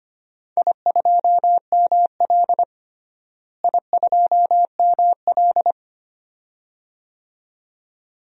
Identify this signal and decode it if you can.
Morse code, heard as a keyed tone, about 25 wpm, 700 Hz. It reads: I2ML I2ML
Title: DXCC-100 Call Signs - 4 Characters - Repeated 2x - 25wpm